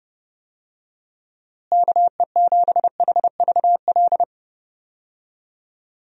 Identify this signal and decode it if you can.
Morse code, heard as a keyed tone, about 29 wpm, 700 Hz. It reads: KE7HVL
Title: USA Call Signs - 6 Characters - Repeated 2x - 30wpm